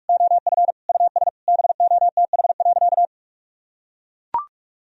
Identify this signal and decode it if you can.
Morse code, heard as a keyed tone, about 43 wpm, 700 Hz. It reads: OF US BOTH.
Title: Sentences from Top 1000 Words - Review Part 4 - 45wpm